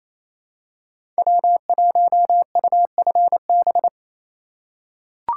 Morse code, heard as a keyed tone, about 28 wpm, 700 Hz. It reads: W1UFB
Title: USA Call Signs - 5 Characters - 6x Character Spacing - 28wpm